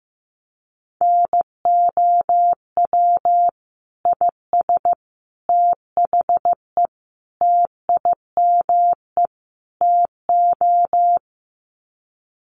Morse code, heard as a keyed tone, about 15 wpm, 700 Hz. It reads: NOW IS THE TIME TO
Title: Sets of 5 Words Encore2 15wpm